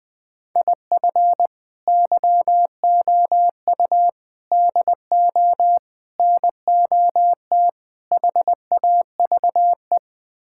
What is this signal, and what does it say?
Morse code, heard as a keyed tone, about 21 wpm, 700 Hz. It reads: IF YOU DO NOT HAVE
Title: Build A Phrase - 5-Word Phrases - 20wpm